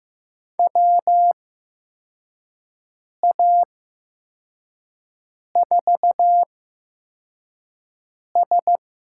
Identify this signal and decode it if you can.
Morse code, heard as a keyed tone, about 15 wpm, spaced as for 4 wpm, 700 Hz. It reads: WA4S
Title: USA Call Signs - 8x Character Spacing - 15wpm